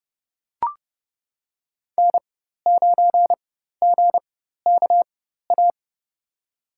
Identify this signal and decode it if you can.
Morse code, heard as a keyed tone, about 29 wpm, spaced as for 14 wpm, 700 Hz. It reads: N9GKA
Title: USA Call Signs - 5 Characters - 4x Character Spacing - 30wpm